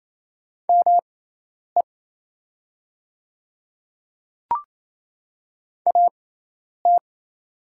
Morse code, heard as a keyed tone, about 28 wpm, spaced as for 10 wpm, 700 Hz. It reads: ME AT
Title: 6x Character Spacing - Top 200 Words - 28wpm